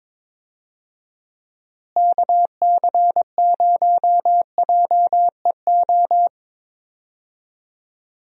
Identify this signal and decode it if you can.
Morse code, heard as a keyed tone, about 22 wpm, 700 Hz. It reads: KC0JEO
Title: USA Call Signs - 6 Characters - Repeated 3x - 22wpm